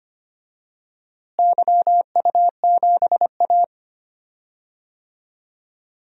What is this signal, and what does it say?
Morse code, heard as a keyed tone, about 25 wpm, 700 Hz. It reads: YU7A
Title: DXCC-100 Call Signs - 4 Characters - Repeated 3x - 25wpm